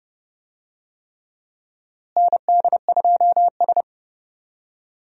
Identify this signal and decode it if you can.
Morse code, heard as a keyed tone, about 30 wpm, 700 Hz. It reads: ND2S
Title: USA Call Signs - 4 Characters - Repeated 3x - 30wpm